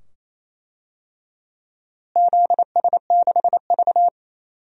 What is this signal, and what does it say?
Morse code, heard as a keyed tone, about 28 wpm, 700 Hz. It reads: ZS6V
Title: DXCC-100 Call Signs - 4 Characters - Repeated 2x - 28wpm